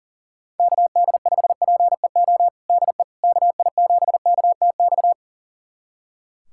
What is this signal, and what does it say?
Morse code, heard as a keyed tone, about 40 wpm, 700 Hz. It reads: KD5PEO DE KI7KTX
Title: USA Callsign DE Callsign - 6 Characters - 40wpm